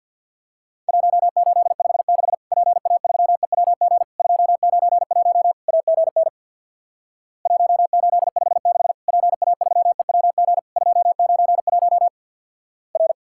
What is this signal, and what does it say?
Morse code, heard as a keyed tone, about 50 wpm, 700 Hz. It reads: 1856 PA3EWG 291 AGN 1856 PA3EWG 291 R
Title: Worked All Europe DX Contest - QTC Exchange with Repeats - 50wpm